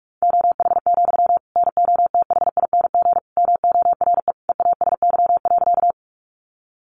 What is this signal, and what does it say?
Morse code, heard as a keyed tone, about 44 wpm, 700 Hz. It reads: OH, NOTHING MORE EASY.
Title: Sentences from Top 700 Words 45wpm